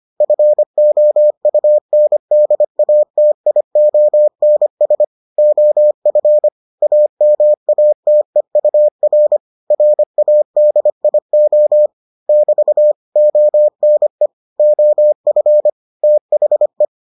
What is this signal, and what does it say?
Morse code, heard as a keyed tone, about 25 wpm, 600 Hz. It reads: FOUNDATIONS OF AMATEUR RADIO <BT> ONE OF THE